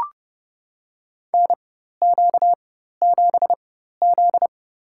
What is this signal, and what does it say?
Morse code, heard as a keyed tone, about 30 wpm, spaced as for 14 wpm, 700 Hz. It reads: NQ7Z
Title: USA Call Signs - 4 Characters - 4x Character Spacing - 30wpm